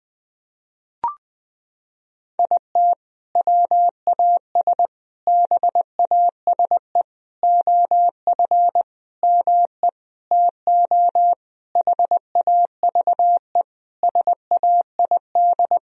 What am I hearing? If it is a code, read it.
Morse code, heard as a keyed tone, about 21 wpm, 700 Hz. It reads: IT WAS BASE OF ME TO HAVE SAID
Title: Sentences from Top 1000 Words - Review Part 1 - 20wpm